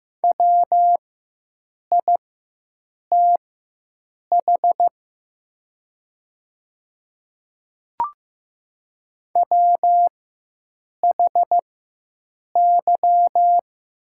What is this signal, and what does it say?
Morse code, heard as a keyed tone, about 15 wpm, spaced as for 7 wpm, 700 Hz. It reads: WITH WHY
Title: 4x Character Spacing - Top 200 Words - 15wpm